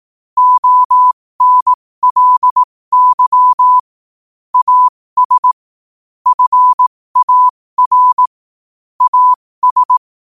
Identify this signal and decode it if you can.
Morse code, heard as a keyed tone, about 18 wpm, spaced as for 15 wpm, 1000 Hz. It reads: ONLY AS FAR AS
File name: qotdfiles / 15wpm